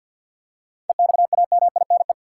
Morse code, heard as a keyed tone, about 50 wpm, 700 Hz. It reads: EXAMINE
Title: Seven Letter Words - 50wpm